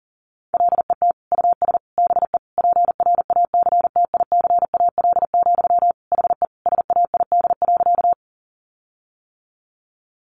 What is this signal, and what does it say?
Morse code, heard as a keyed tone, about 40 wpm, 700 Hz. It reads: LET US BE PRACTICAL, HE SAID.
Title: Sentences from Top 1400 Words 40wpm